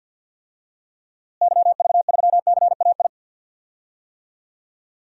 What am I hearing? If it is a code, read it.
Morse code, heard as a keyed tone, about 52 wpm, 700 Hz. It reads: YV3CAI
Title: DXCC-100 Call Signs - 6 Characters - Repeated 3x - 50wpm